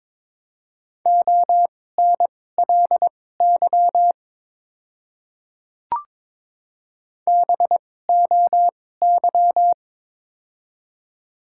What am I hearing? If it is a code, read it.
Morse code, heard as a keyed tone, about 22 wpm, spaced as for 16 wpm, 700 Hz. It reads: ONLY BOY
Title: 2x Character Spacing - Top 200 Words - 22wpm